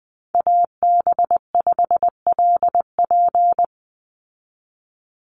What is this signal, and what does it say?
Morse code, heard as a keyed tone, about 20 wpm, 700 Hz. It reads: AB5LP
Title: USA Call Signs 20wpm